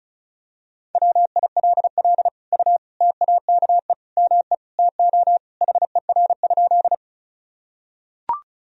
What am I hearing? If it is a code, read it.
Morse code, heard as a keyed tone, about 36 wpm, 700 Hz. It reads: WILL U TAKE ME TO HER?
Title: Sentences from Top 200 Words and Abbreviations 35wpm